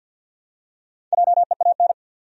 Morse code, heard as a keyed tone, about 47 wpm, 700 Hz. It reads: JEAN
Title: USA Common Names - Max 5 Letters - 50wpm